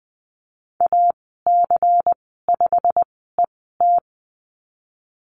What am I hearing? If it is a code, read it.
Morse code, heard as a keyed tone, about 20 wpm, spaced as for 14 wpm, 700 Hz. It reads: AC5ET
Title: 238 Lesson 38g - Letter q - Call Signs 20wpm